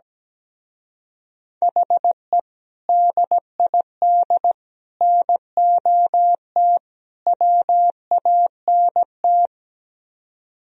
Morse code, heard as a keyed tone, about 17 wpm, 700 Hz. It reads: HE DID NOT WANT
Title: Speed-Racing - Sets of 4 Words 17wpm